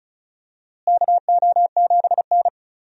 Morse code, heard as a keyed tone, about 36 wpm, 700 Hz. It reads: KO7N
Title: USA Call Signs - 4 Characters - Repeated 3x - 35wpm